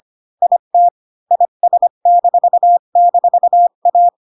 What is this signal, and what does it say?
Morse code, heard as a keyed tone, about 25 wpm, 700 Hz. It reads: IT IS--A
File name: Pinocchio / Pinocchio ch1 20 25 ik1hge